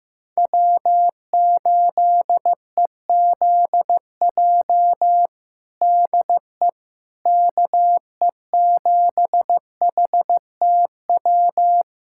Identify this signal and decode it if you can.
Morse code, heard as a keyed tone, about 16 wpm, 700 Hz. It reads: W8EZJ DE KE7HTW
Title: USA Callsign DE Callsign 15wpm